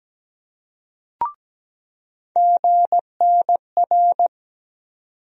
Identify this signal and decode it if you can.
Morse code, heard as a keyed tone, about 17 wpm, 700 Hz. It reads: GNR
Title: USA Call Sign Postfixes - 3 Letters 17wpm